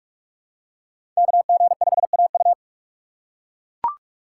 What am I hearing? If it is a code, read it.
Morse code, heard as a keyed tone, about 44 wpm, 700 Hz. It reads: KG5AU
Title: USA Call Signs - 5 Characters - 6x Character Spacing - 45wpm